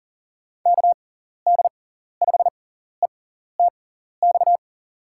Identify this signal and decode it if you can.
Morse code, heard as a keyed tone, about 40 wpm, spaced as for 14 wpm, 700 Hz. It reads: KD5ETX
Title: USA Call Signs - 6 Characters - 6x Character Spacing - 40wpm